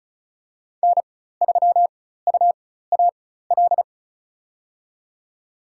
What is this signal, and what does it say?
Morse code, heard as a keyed tone, about 34 wpm, spaced as for 16 wpm, 700 Hz. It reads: N3UAL
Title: USA Call Signs - 5 Characters - 4x Character Spacing - 35wpm